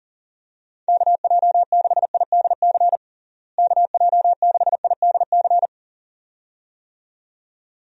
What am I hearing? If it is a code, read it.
Morse code, heard as a keyed tone, about 40 wpm, 700 Hz. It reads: KJ6IDC KJ6IDC
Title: USA Call Signs - 6 Characters - Repeated 2x - 40wpm